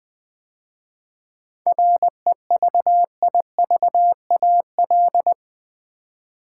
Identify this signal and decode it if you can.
Morse code, heard as a keyed tone, about 20 wpm, 700 Hz. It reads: REVIVAL
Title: 7 Letter Words - First 3 Letters Given - 20wpm